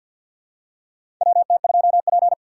Morse code, heard as a keyed tone, about 52 wpm, 700 Hz. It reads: WT2P
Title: USA Call Signs - 4 Characters - 3x Character Spacing - 50wpm